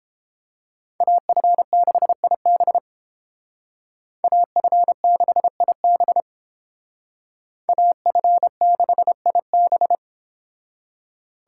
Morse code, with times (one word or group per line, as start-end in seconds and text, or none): 1.0-2.8 AF6IB
4.2-6.2 AF6IB
7.7-10.0 AF6IB